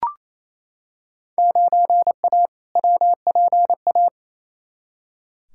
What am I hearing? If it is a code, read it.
Morse code, heard as a keyed tone, about 28 wpm, 700 Hz. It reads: 9A WPA
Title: Field Day Exchanges Encore - 28wpm